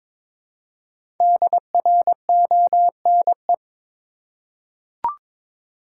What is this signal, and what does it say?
Morse code, heard as a keyed tone, about 22 wpm, 700 Hz. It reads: DRONE
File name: practice / 22wpm / Difficult Words with Hints - 22wpm